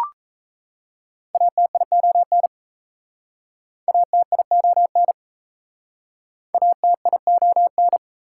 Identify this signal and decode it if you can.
Morse code, heard as a keyed tone, about 39 wpm, 700 Hz. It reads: ATION ATION ATION